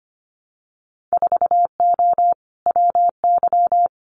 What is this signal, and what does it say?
Morse code, heard as a keyed tone, about 25 wpm, 690 Hz. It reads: 4O WY
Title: Winter Field Day Exchanges Encore 25wpm